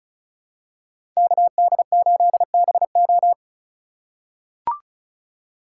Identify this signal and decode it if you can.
Morse code, heard as a keyed tone, about 36 wpm, 700 Hz. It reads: KD8BO
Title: USA Call Signs - 8x Character Spacing - 35wpm